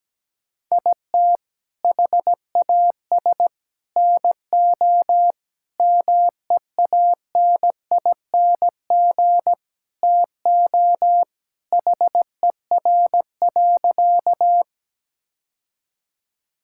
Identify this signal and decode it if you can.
Morse code, heard as a keyed tone, about 17 wpm, 700 Hz. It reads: IT HAS NO MEANING TO HER.
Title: Sentences from Top 1000 Words - Review Part 6 - 17wpm